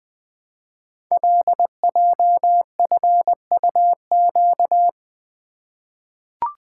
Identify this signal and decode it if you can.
Morse code, heard as a keyed tone, about 20 wpm, 700 Hz. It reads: LJFUQ
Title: Code Groups - 5 Letters - 20wpm